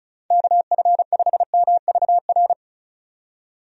Morse code, heard as a keyed tone, about 36 wpm, 700 Hz. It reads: KF5MVR
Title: USA Call Signs - 6 Characters - 35wpm